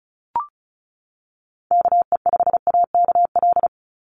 Morse code, heard as a keyed tone, about 34 wpm, 700 Hz. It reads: KE5AKL
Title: NA SOTA Activator Callsigns 35wpm